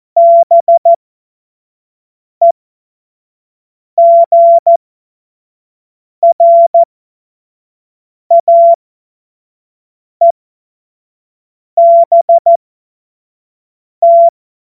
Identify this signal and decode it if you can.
Morse code, heard as a keyed tone, about 14 wpm, spaced as for 5 wpm, 680 Hz. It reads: BEGRAEBT
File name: Morsekurs / Witze / Witz 14.November 2023 25 F70